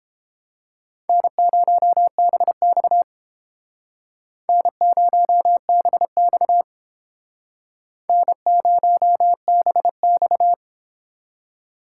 Morse code, heard as a keyed tone, about 30 wpm, 700 Hz. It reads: N0BX N0BX N0BX